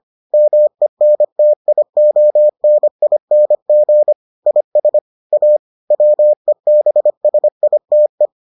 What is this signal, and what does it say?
Morse code, heard as a keyed tone, about 25 wpm, 600 Hz. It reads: MENTIONING IS A WEBSITE